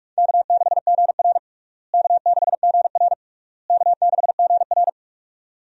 Morse code, heard as a keyed tone, about 46 wpm, 700 Hz. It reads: K6GR K6GR K6GR
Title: USA Call Signs - 4 Characters - Repeated 3x - 45wpm